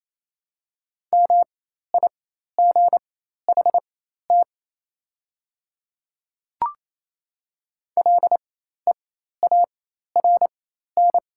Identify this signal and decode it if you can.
Morse code, heard as a keyed tone, about 28 wpm, spaced as for 13 wpm, 700 Hz. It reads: MIGHT LEARN
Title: 4x Character Spacing - Top 500 Words - 28wpm